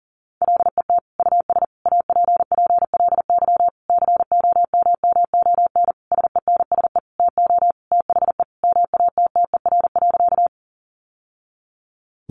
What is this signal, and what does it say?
Morse code, heard as a keyed tone, about 40 wpm, 700 Hz. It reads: LET US APPLY COMMON SENSE TO THE MATTER.
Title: Sentences from Top 1200 Words 40wpm